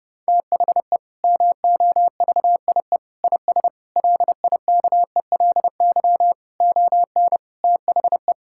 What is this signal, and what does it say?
Morse code, heard as a keyed tone, about 32 wpm, 700 Hz. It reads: THE MOVIE IS LIKELY ON THE
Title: Sentences from Top 1000 Words - Review Part 5 - 30wpm